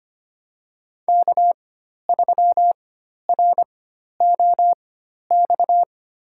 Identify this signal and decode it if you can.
Morse code, heard as a keyed tone, about 25 wpm, spaced as for 12 wpm, 700 Hz. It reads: K3ROX
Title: USA Call Signs - 5 Characters - 4x Character Spacing - 25wpm